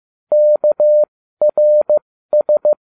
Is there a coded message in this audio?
Yes: Morse code from 0.3 to 2.7 s, heard as a keyed tone, about 15 wpm, spaced as for 13 wpm, 600 Hz. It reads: KRS